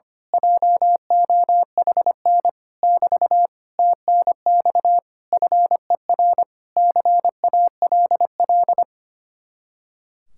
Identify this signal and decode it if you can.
Morse code, heard as a keyed tone, about 24 wpm, 700 Hz. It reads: JOHN <BT> TNX FER CALL
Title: Speed-Racing Common QSO Phrases - 25wpm